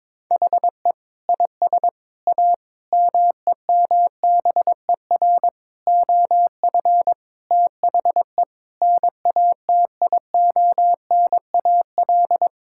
Morse code, heard as a keyed tone, about 22 wpm, 700 Hz. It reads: HE IS A MEMBER OF THE NATIONAL
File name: practice / 22wpm / Sentences with 3-Word Phrases - 22wpm